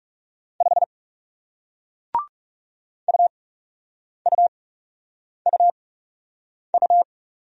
Morse code, heard as a keyed tone, about 37 wpm, spaced as for 8 wpm, 700 Hz. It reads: 5 UUUU
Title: Speed-Racing Single Letter-Number 30wpm